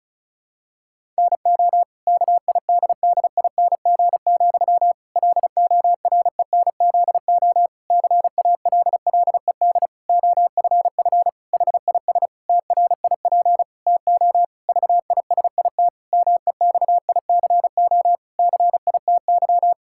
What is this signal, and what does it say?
Morse code, heard as a keyed tone, about 36 wpm, 700 Hz. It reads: NO KIDDING, LORENZO CALLED OFF HIS TRIP TO VISIT MEXICO CITY
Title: Pangram Challenge v2 - 35wpm